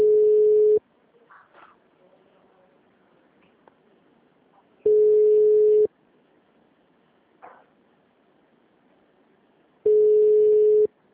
Ringback tone (Europe, 425 Hz).
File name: out- -2010- - - .75